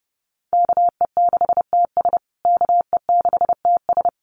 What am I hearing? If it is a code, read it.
Morse code, heard as a keyed tone, about 30 wpm, 700 Hz. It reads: KE6TS KE6TS
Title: USA Call Signs Repeated 3x 30wpm